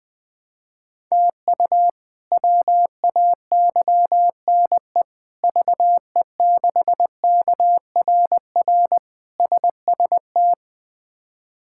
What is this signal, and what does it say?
Morse code, heard as a keyed tone, about 20 wpm, 700 Hz. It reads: TU WAYNE VE6KRR SST
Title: K1USN SST QSO Practice - 20wpm